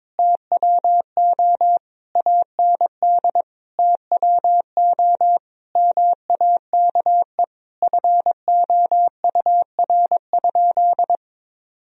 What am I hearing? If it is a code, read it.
Morse code, heard as a keyed tone, about 22 wpm, 700 Hz. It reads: TWO AND TWO MAKE FOUR?
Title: Sentences from Top 1000 Words - Review Part 1 - 22wpm